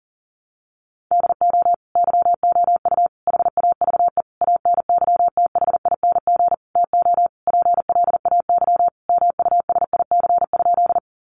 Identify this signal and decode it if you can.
Morse code, heard as a keyed tone, about 40 wpm, 700 Hz. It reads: DO YOU HAVE ANYTHING TO PLAY MUSIC?
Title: Sentences from Top 1000 Words 40wpm